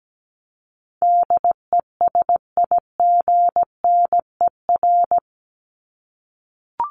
Morse code, heard as a keyed tone, about 17 wpm, 700 Hz. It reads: DESIGNER
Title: Top 1501-1750 Words 17wpm